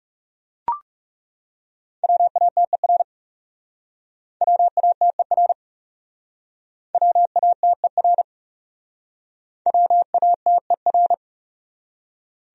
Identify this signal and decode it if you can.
Morse code, heard as a keyed tone, about 38 wpm, 700 Hz. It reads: WATER WATER WATER WATER